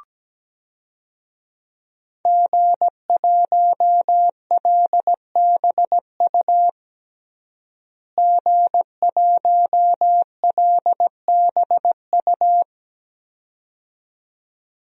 Morse code, heard as a keyed tone, about 17 wpm, 700 Hz. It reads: G1LBU G1LBU
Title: DXCC-100 Call Signs - 5 Characters - Repeated 2x - 17wpm